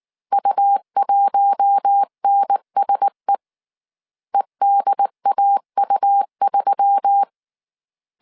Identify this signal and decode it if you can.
Morse code, heard as a keyed tone, about 19 wpm, 800 Hz. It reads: U1NSE EDAU3
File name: CW / DET / 1 6 1500C S C 18wpm 2356